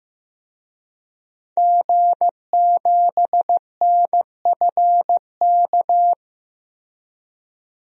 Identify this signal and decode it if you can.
Morse code, heard as a keyed tone, about 15 wpm, 700 Hz. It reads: G7NFK
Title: DXCC-100 Call Signs - 5 Characters - Repeated 3x - 15wpm